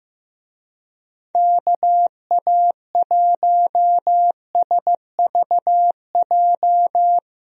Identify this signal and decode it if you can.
Morse code, heard as a keyed tone, about 15 wpm, 700 Hz. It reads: KA1SVJ
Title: USA Call Signs - 6 Characters - Repeated 2x - 15wpm